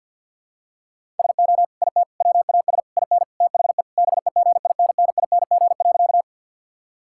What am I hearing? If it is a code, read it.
Morse code, heard as a keyed tone, about 48 wpm, 700 Hz. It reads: SO IT WAS IN THE BEGINNING.
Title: Sentences from Top 1000 Words - Review Part 5 - 50wpm